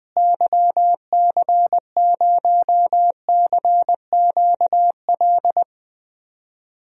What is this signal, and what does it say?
Morse code, heard as a keyed tone, about 20 wpm, 700 Hz. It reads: YC0CQL
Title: DXCC-100 Call Signs - 6 Characters - Repeated 3x - 20wpm